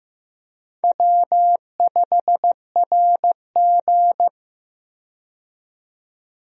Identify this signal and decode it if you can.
Morse code, heard as a keyed tone, about 15 wpm, 700 Hz. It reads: W5RG
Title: USA Call Signs - 4 Characters - 15wpm